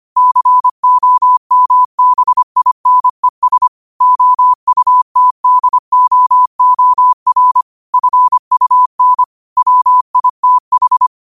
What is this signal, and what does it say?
Morse code, heard as a keyed tone, about 25 wpm, 1000 Hz. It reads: COMBINES OUTDOOR FUN WITH
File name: qotdfiles / 25wpm